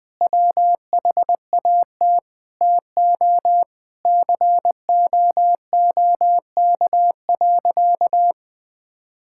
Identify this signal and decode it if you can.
Morse code, heard as a keyed tone, about 21 wpm, 700 Hz. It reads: WHAT TO COOK.
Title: Sentences with 5-Word Phrases - 20wpm